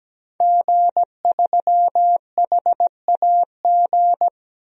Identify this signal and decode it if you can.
Morse code, heard as a keyed tone, about 17 wpm, 700 Hz. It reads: G3HAG
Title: DXCC-100 Call Signs - 5 Characters - 17wpm